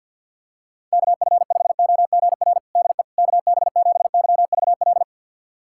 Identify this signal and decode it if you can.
Morse code, heard as a keyed tone, about 48 wpm, 700 Hz. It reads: KF5OGR DE KB7YVL
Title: USA Callsign DE Callsign - 6 Characters - 50wpm